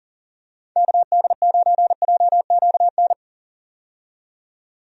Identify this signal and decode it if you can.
Morse code, heard as a keyed tone, about 40 wpm, 700 Hz. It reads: KD9JQN
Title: USA Call Signs - 6 Characters - 40wpm